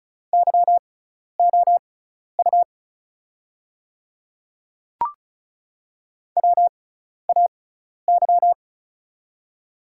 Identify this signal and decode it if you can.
Morse code, heard as a keyed tone, about 36 wpm, spaced as for 12 wpm, 690 Hz. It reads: YOU WAY
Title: 6x Character Spacing - Top 200 Words - 35wpm